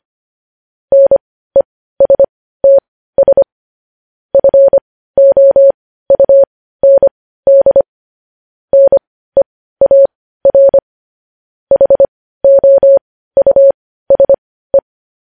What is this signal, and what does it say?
Morse code, heard as a keyed tone, about 25 wpm, spaced as for 15 wpm, 550 Hz. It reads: NESTS FOUND NEAR HOUSE
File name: podcast / News headlines-1x-25x15